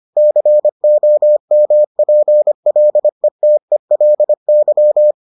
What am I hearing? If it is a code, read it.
Morse code, heard as a keyed tone, about 25 wpm, 600 Hz. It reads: COMPLETELY